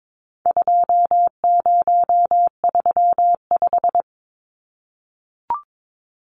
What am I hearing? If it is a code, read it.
Morse code, heard as a keyed tone, about 22 wpm, 700 Hz. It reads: 2035
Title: Numbers - 4-Digit 22wpm